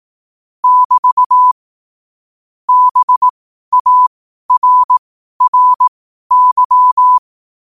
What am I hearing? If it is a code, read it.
Morse code, heard as a keyed tone, about 18 wpm, spaced as for 12 wpm, 1000 Hz. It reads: <BT> BARRY
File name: qotdfiles / 12wpm